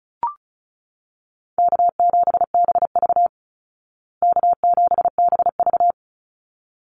Morse code, heard as a keyed tone, about 34 wpm, 700 Hz. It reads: K7BV K7BV
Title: Call Signs Repeated 3x - 35wpm